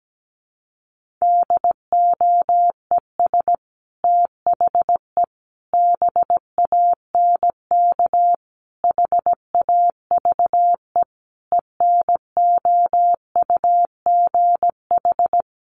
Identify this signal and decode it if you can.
Morse code, heard as a keyed tone, about 17 wpm, 700 Hz. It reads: DOES THE BANK HAVE ENOUGH
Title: Sentences from Top 1500 Words 17wpm